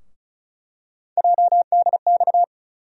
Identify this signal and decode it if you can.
Morse code, heard as a keyed tone, about 34 wpm, 700 Hz. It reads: JDX
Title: USA Call Sign Postfixes - 3 Letters 35wpm